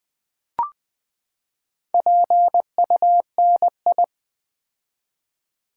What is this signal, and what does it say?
Morse code, heard as a keyed tone, about 20 wpm, 700 Hz. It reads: PUNI
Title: Build A Word - 4 Letters At A Time - 10-Letter Words - 20wpm